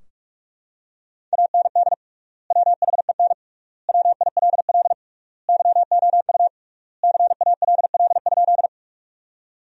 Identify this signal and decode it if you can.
Morse code, heard as a keyed tone, about 41 wpm, 700 Hz. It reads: AND WHEN WILL YOU CALL?
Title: 3x Word-Spacing - Sentences from Top 100 Words - 45wpm